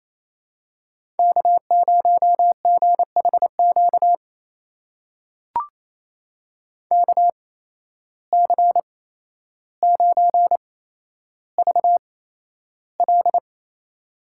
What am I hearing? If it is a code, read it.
Morse code, heard as a keyed tone, about 28 wpm, spaced as for 12 wpm, 700 Hz. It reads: K0GHQ KC9VL
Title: USA Call Signs - 5 Characters - 8x Character Spacing - 28wpm